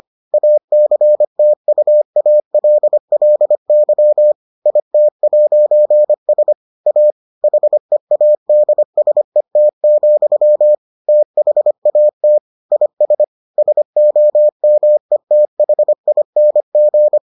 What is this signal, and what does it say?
Morse code, heard as a keyed tone, about 25 wpm, 600 Hz. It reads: ACTUALLY IT'S A HEADSET, THAT IS SOMETHING